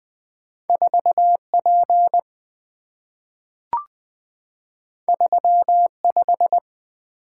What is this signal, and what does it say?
Morse code, heard as a keyed tone, about 20 wpm, 700 Hz. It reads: 4P 35
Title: Two Letter-Number Combinations - 20wpm